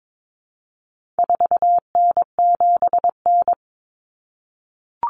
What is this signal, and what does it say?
Morse code, heard as a keyed tone, about 22 wpm, 700 Hz. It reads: 4N7N
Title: Common Cut Numbers - 4-Digit 22wpm